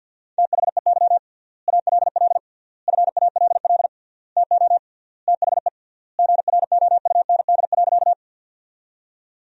Morse code, heard as a keyed tone, about 52 wpm, 700 Hz. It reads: THEY ALL FALL TO THE GROUND.
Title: 3x Word-Spacing - Sentences from Top 500 Words - 50wpm